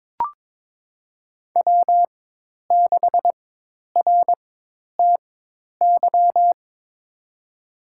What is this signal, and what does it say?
Morse code, heard as a keyed tone, about 22 wpm, spaced as for 10 wpm, 700 Hz. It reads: W6RTY
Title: USA Call Signs - 5 Characters - 4x Character Spacing - 22wpm